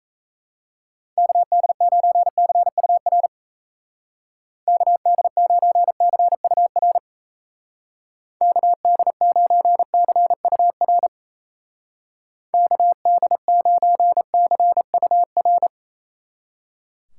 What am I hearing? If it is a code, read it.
Morse code, heard as a keyed tone, about 34 wpm, 700 Hz. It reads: KD9CUR KD9CUR KD9CUR KD9CUR
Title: Speed Racing - USA 2x3 Call Signs 28wpm